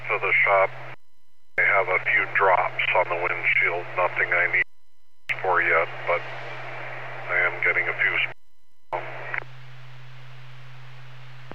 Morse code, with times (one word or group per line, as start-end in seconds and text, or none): none